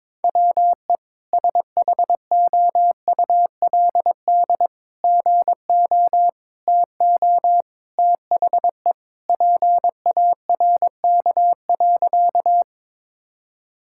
Morse code, heard as a keyed tone, about 22 wpm, 700 Hz. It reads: WE SHOULD GO TO THE PARK.
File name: practice / 22wpm / Sentences with 3-Word Phrases - 22wpm